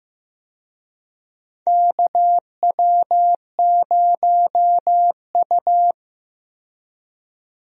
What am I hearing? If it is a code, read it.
Morse code, heard as a keyed tone, about 15 wpm, 700 Hz. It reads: KW0U
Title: USA Call Signs - 4 Characters - 15wpm